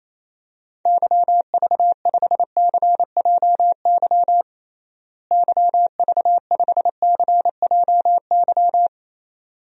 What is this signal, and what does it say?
Morse code, heard as a keyed tone, about 28 wpm, 700 Hz. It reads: YV5CJY YV5CJY
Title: DXCC-100 Call Signs - 6 Characters - Repeated 2x - 28wpm